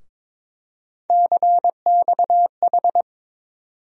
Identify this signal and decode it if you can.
Morse code, heard as a keyed tone, about 22 wpm, 700 Hz. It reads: CXH
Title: USA Call Sign Postfixes - 3 Letters 22wpm